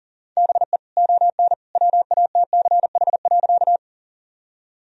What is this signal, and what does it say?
Morse code, heard as a keyed tone, about 40 wpm, 700 Hz. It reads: BE ON WATCH.
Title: Sentences from Top 1000 Words - Review Part 1 - 40wpm